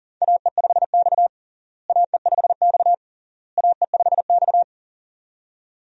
Morse code, heard as a keyed tone, about 40 wpm, 700 Hz. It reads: AE5X AE5X AE5X
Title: USA Call Signs - 4 Characters - Repeated 3x - 40wpm